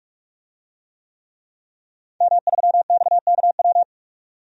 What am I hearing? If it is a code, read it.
Morse code, heard as a keyed tone, about 43 wpm, 700 Hz. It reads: M3XKW
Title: DXCC-100 Call Signs - 5 Characters - 45wpm